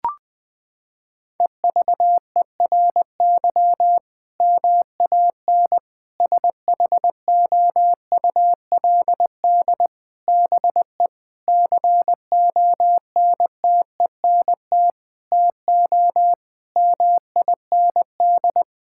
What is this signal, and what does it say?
Morse code, heard as a keyed tone, about 21 wpm, 700 Hz. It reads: EVERY MAN SHOULD BE CONTENT TO MIND
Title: Sentences from Top 1000 Words - Review Part 6 - 20wpm